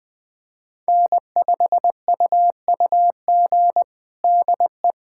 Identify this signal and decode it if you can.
Morse code, heard as a keyed tone, about 20 wpm, 700 Hz. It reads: N5UUG DE
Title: USA Callsign DE Callsign - 5 Characters - 20wpm